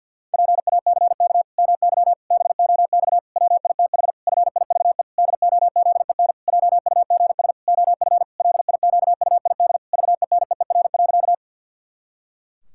Morse code, heard as a keyed tone, about 49 wpm, 700 Hz. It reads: PACK MY BOX WITH FIVE DOZEN JUGS OF LIQUID VENEER.
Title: Pangram Challenge v2 - 50wpm